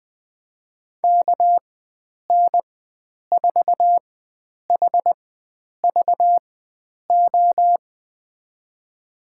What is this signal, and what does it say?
Morse code, heard as a keyed tone, about 20 wpm, spaced as for 9 wpm, 700 Hz. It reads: KN4HVO
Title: USA Call Signs - 6 Characters - 4x Character Spacing - 20wpm